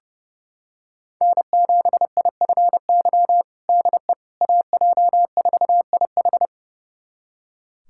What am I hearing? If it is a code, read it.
Morse code, heard as a keyed tone, about 31 wpm, 700 Hz. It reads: N7IFY DE AJ4IH
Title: USA Callsign DE Callsign - 5 Characters - 30wpm